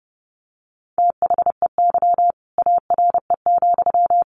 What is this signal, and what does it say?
Morse code, heard as a keyed tone, about 28 wpm, 700 Hz. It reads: THEY ARE,
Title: Sentences from Top 1200 Words 30wpm